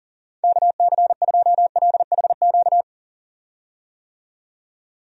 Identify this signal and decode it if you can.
Morse code, heard as a keyed tone, about 40 wpm, 700 Hz. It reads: KC2LHQ
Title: USA Call Signs - 6 Characters - 40wpm